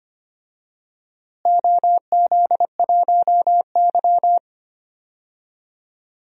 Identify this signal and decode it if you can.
Morse code, heard as a keyed tone, about 25 wpm, 700 Hz. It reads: OZ1Y
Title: DXCC-100 Call Signs - 4 Characters - Repeated 3x - 25wpm